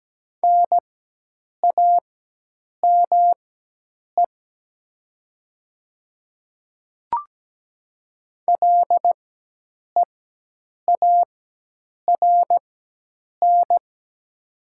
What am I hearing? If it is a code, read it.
Morse code, heard as a keyed tone, about 17 wpm, spaced as for 8 wpm, 700 Hz. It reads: NAME LEARN
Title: 4x Character Spacing - Top 500 Words - 17wpm